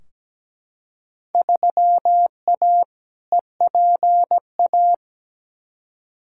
Morse code, heard as a keyed tone, about 17 wpm, 700 Hz. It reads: 3A EPA
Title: Field Day Exchanges - 17wpm